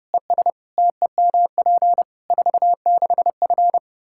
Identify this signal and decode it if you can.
Morse code, heard as a keyed tone, about 30 wpm, 700 Hz. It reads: ES TEMP 46F
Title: Practice QSO - Exchange 2 - Info - 30wpm